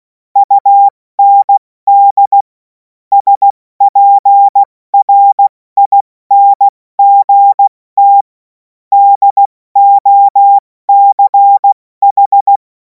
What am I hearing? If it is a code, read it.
Morse code, heard as a keyed tone, about 16 wpm, 800 Hz. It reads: UND SPRINGT DOCH
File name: Morsekurs / Witze / Witz 11. August 2026 70 F80